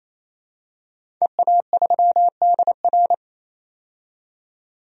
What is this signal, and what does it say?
Morse code, heard as a keyed tone, about 27 wpm, 700 Hz. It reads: EA3DR
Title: DXCC-100 Call Signs - 5 Characters - 28wpm